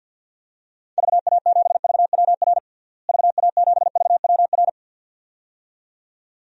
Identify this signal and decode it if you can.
Morse code, heard as a keyed tone, about 52 wpm, 700 Hz. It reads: VA7VWR VA7VWR
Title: DXCC-100 Call Signs - 6 Characters - Repeated 2x - 50wpm